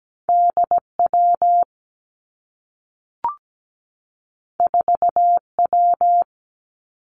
Morse code, heard as a keyed tone, about 17 wpm, 700 Hz. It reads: D W 4 W
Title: Two Challenging Letter-Number Combinations 17wpm